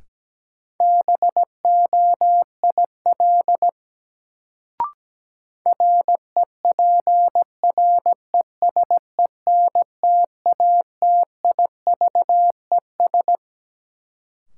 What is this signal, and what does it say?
Morse code, heard as a keyed tone, about 17 wpm, 700 Hz. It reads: BOIL REPRESENTATIVES
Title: Sets of 5 Words Encore - New Word Drill - 17wpm